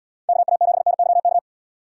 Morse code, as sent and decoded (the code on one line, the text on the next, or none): -.- .- --... -. -.-- -.-
KA7NYK